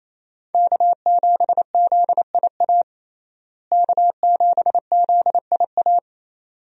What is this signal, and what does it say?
Morse code, heard as a keyed tone, about 28 wpm, 700 Hz. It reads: K7ZIA K7ZIA
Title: USA Call Signs - 5 Characters - Repeated 2x - 28wpm